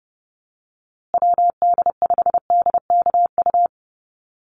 Morse code, heard as a keyed tone, about 30 wpm, 700 Hz. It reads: WD5DKU
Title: USA Call Signs Repeated 3x 30wpm